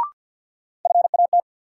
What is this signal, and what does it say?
Morse code, heard as a keyed tone, about 51 wpm, 700 Hz. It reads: FAT
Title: Sentences from Top 600 Words - New Word Drill - 50wpm